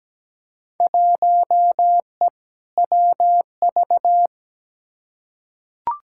Morse code, heard as a keyed tone, about 17 wpm, 700 Hz. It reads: 1E WV
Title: Field Day Exchanges Encore - 17wpm